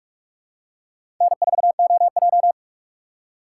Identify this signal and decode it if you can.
Morse code, heard as a keyed tone, about 45 wpm, 700 Hz. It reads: N4OJ
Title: USA Call Signs - 2x Character Spacing - 45wpm